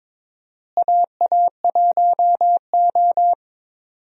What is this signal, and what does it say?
Morse code, heard as a keyed tone, about 22 wpm, 700 Hz. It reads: AA1O
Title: USA Call Signs - 4 Characters - Repeated 2x - 22wpm